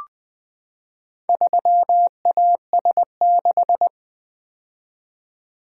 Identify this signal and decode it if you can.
Morse code, heard as a keyed tone, about 20 wpm, 700 Hz. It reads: 3AS6
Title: Four Letter-Number Combinations - 20wpm